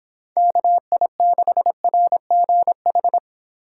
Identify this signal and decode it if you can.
Morse code, heard as a keyed tone, about 25 wpm, 700 Hz. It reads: KI6RGH